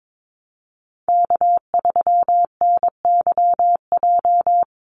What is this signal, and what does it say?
Morse code, heard as a keyed tone, about 22 wpm, 700 Hz. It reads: K3NYJ
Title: USA Call Signs Repeated 3x 22wpm